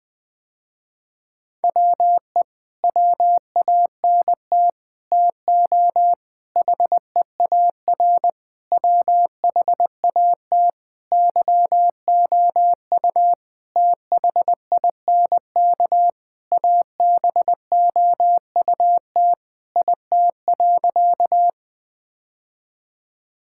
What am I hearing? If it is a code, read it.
Morse code, heard as a keyed tone, about 20 wpm, 700 Hz. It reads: WE WANT TO HEAR WHAT YOU THINK ABOUT IT.
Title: Sentences with 7-Word Phrases - Spoken Beginning - 20wpm